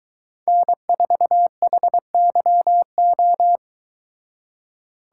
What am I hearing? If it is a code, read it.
Morse code, heard as a keyed tone, about 23 wpm, 700 Hz. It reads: N4HYO